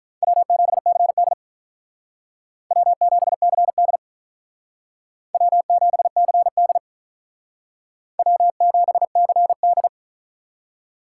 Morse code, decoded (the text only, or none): W7CD W7CD W7CD W7CD